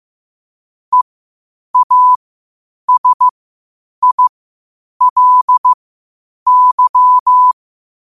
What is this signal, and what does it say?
Morse code, heard as a keyed tone, about 15 wpm, spaced as for 8 wpm, 1000 Hz. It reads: EASILY